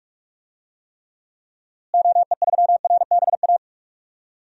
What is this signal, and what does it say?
Morse code, heard as a keyed tone, about 45 wpm, 700 Hz. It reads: OE3RBA
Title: DXCC-100 Call Signs - 6 Characters - Repeated 3x - 45wpm